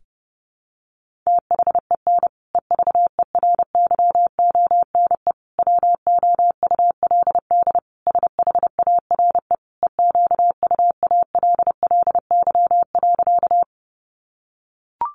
Morse code, heard as a keyed tone, about 30 wpm, 700 Hz. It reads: THEN EVERYONE WOULD SHARE EQUALLY.
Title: Sentences from Top 1400 Words 30wpm